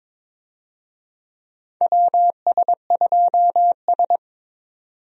Morse code, heard as a keyed tone, about 22 wpm, 700 Hz. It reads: WS2S
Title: USA Call Signs - 4 Characters - 22wpm